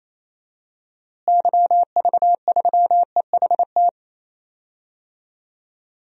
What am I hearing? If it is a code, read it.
Morse code, heard as a keyed tone, about 28 wpm, 700 Hz. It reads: YV3EHT
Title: DXCC-100 Call Signs - 6 Characters - Repeated 3x - 28wpm